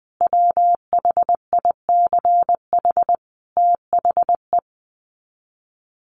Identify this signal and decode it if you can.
Morse code, heard as a keyed tone, about 20 wpm, 700 Hz. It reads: WHICH THE
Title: Sets of 5 Words Encore2 20wpm